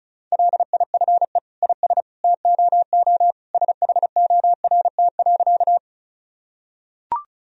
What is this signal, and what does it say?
Morse code, heard as a keyed tone, about 36 wpm, 700 Hz. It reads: LIFE IS TOO SHORT.
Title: Sentences from Top 1000 Words - Review Part 2 - 35wpm